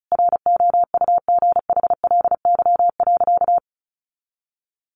Morse code, heard as a keyed tone, about 34 wpm, 700 Hz. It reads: ROUGHLY.
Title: Sentences from Top 1500 Words 35wpm